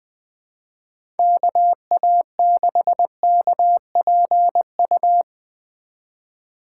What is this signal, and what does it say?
Morse code, heard as a keyed tone, about 21 wpm, 700 Hz. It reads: KA6KPU
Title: USA Call Signs - 6 Characters - 8x Character Spacing - 20wpm